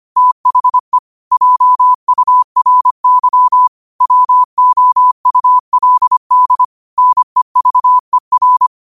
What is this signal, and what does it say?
Morse code, heard as a keyed tone, about 25 wpm, 1000 Hz. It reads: THE JURY WOULD NEVER
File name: qotdfiles / 25wpm